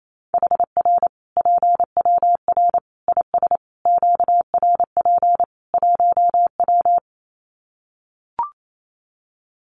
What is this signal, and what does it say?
Morse code, heard as a keyed tone, about 28 wpm, 700 Hz. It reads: HR PWR IS QRP 1W
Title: Common QSO Phrases 28wpm